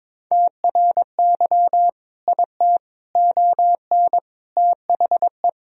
Morse code, heard as a keyed tone, about 21 wpm, 700 Hz. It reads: TRY IT ON THE
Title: Sentences from Top 1000 Words - Review Part 2 - 22wpm